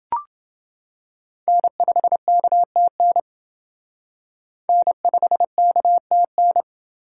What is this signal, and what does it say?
Morse code, heard as a keyed tone, about 31 wpm, 700 Hz. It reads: N5KTN N5KTN